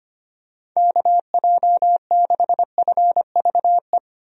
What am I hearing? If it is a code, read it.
Morse code, heard as a keyed tone, about 25 wpm, 700 Hz. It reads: KJ6FVE
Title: USA Call Signs - 6 Characters - Repeated 3x - 25wpm